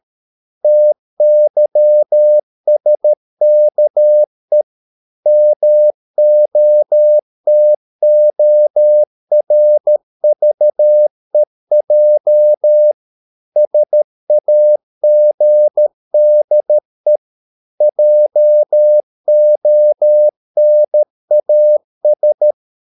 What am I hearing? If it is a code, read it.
Morse code, heard as a keyed tone, about 13 wpm, 600 Hz. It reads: TYSKE MOTORVEJ SAGDE JONAS
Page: Ferie 13 wpm